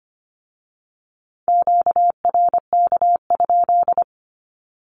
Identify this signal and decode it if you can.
Morse code, heard as a keyed tone, about 25 wpm, 700 Hz. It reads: QRK?
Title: Q-Codes with Spoken Meaning - 25wpm